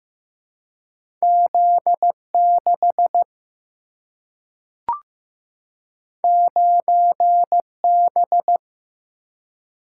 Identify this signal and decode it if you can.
Morse code, heard as a keyed tone, about 15 wpm, 700 Hz. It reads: Z 6 9 B
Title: Single-Letter Single-Number 15wpm